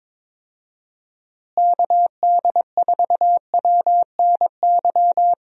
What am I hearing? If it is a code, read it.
Morse code, heard as a keyed tone, about 22 wpm, 700 Hz. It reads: KD4WNY
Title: USA Call Signs - 6 Characters - Repeated 2x - 22wpm